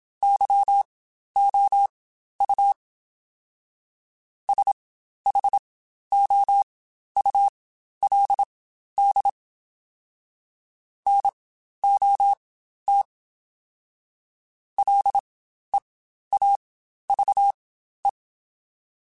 Morse code, heard as a keyed tone, about 27 wpm, spaced as for 12 wpm, 800 Hz. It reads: YOU SHOULD NOT LEAVE
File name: Spaced / Chuckles Spaced-mp3s / Temptation sp @25